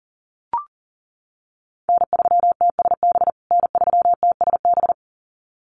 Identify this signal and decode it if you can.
Morse code, heard as a keyed tone, about 40 wpm, 700 Hz. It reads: N3TSB N3TSB
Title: USA Call Signs Repeated 2x 40wpm